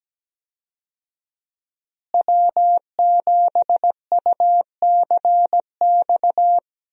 Morse code, heard as a keyed tone, about 17 wpm, 700 Hz. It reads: W7UCX
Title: USA Call Signs - 5 Characters - Repeated 3x - 17wpm